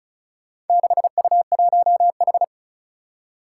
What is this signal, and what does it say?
Morse code, heard as a keyed tone, about 36 wpm, 700 Hz. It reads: 6U1H
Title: Four Challenging Letter-Number Combinations 35wpm